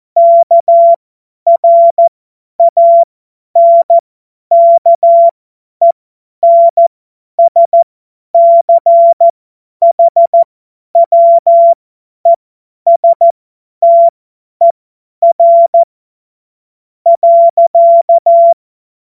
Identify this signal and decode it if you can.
Morse code, heard as a keyed tone, about 14 wpm, spaced as for 10 wpm, 680 Hz. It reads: KRANKENSCHWESTER.
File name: Morsekurs / Witze / Witz 04. Juli 2023 50 F70